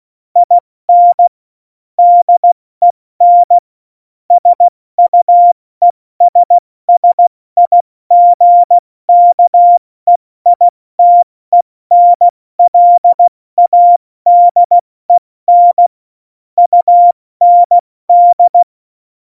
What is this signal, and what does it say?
Morse code, heard as a keyed tone, about 16 wpm, 700 Hz. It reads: IN DEN SUESSIGKEITENLADEN UND
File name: Morsekurs / Witze / Witz 14 April 2020, 70 F80